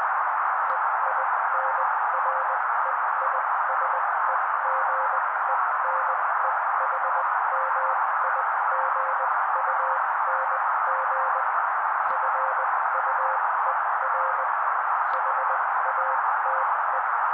Morse code, heard as a keyed tone, about 20 wpm, spaced as for 16 wpm, 530 Hz. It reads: EINREISEGENEHMIGUNG FUER HATE
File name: daily / 16wpm noise